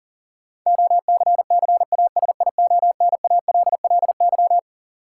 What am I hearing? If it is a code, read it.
Morse code, heard as a keyed tone, about 40 wpm, 700 Hz. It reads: OCCASIONALLY
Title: Sentences from Top 1300 Words - New Word Drill - 40wpm